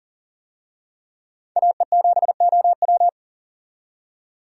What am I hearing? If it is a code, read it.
Morse code, heard as a keyed tone, about 39 wpm, 700 Hz. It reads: AE7OW
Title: USA Call Signs - 5 Characters - 40wpm